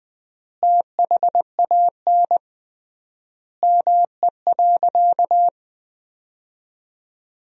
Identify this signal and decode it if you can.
Morse code, heard as a keyed tone, about 21 wpm, 700 Hz. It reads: THAN ME.
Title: 3x Word-Spacing - Sentences from Top 400 Words - 20wpm